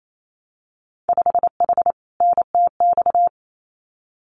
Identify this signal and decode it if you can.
Morse code, heard as a keyed tone, about 28 wpm, 700 Hz. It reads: 5H NTX
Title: Winter Field Day Exchanges Encore 28wpm